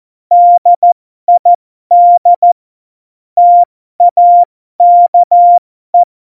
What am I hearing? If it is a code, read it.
Morse code, heard as a keyed tone, about 14 wpm, spaced as for 12 wpm, 700 Hz. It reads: DID TAKE